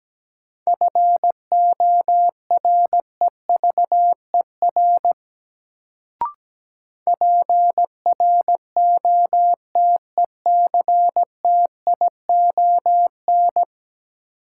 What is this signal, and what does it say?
Morse code, heard as a keyed tone, about 17 wpm, 700 Hz. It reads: FOREVER PROTECTION
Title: Sentences from Top 1600 Words - New Word Drill - 17wpm